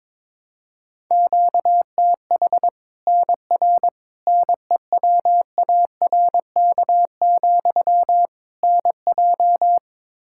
Morse code, heard as a keyed tone, about 22 wpm, 700 Hz. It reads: QTH NR NEWARK, NJ
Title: Practice QSO - Exchange 1 - Beginning - 22wpm